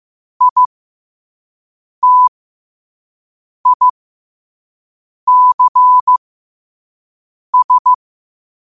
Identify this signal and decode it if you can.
Morse code, heard as a keyed tone, about 15 wpm, spaced as for 5 wpm, 1000 Hz. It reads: ITICS
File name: qotdfiles / 5wpm